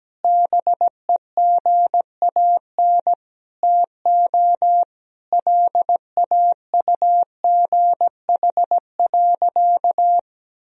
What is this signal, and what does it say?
Morse code, heard as a keyed tone, about 17 wpm, 700 Hz. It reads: BEGAN TO LAUGH.
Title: Sentences from Top 1000 Words - Review Part 2 - 17wpm